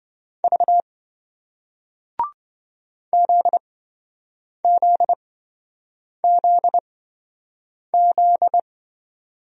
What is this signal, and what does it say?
Morse code, heard as a keyed tone, about 28 wpm, spaced as for 7 wpm, 700 Hz. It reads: V ZZZZ